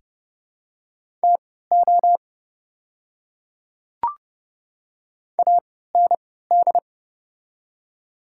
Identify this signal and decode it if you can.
Morse code, heard as a keyed tone, about 30 wpm, spaced as for 17 wpm, 700 Hz. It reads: TO AND
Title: 3x Character Spacing - Top 200 Words - 30wpm